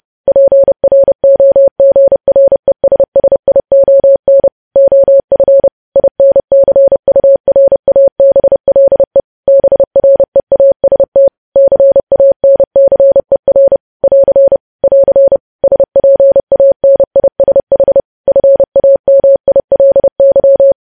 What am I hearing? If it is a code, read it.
Morse code, heard as a keyed tone, about 30 wpm, 550 Hz. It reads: PROGRESSION OF INCURABLE BREAST CANCER <AR> <AR> SPANISH FAMILY
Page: news headlines in morse code at 30 wpm